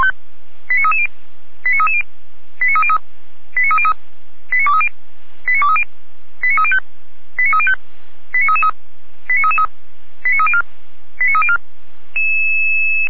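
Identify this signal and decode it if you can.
Five-tone selective call (ZVEI), heard as quick runs of five tones, tones of about 70 ms. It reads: 89330 89330 89303 89303 89230 89230 89305 89305 89333 89304 89304
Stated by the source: ZVEI five-tone selective call 89319 received by SDR, recorded by Feuerwehr Orlamünde.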